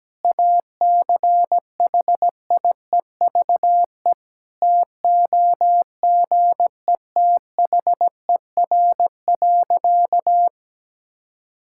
Morse code, heard as a keyed tone, about 17 wpm, 700 Hz. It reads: ACHIEVE TOGETHER.
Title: Sentences from Top 1000 Words - Review Part 6 - 17wpm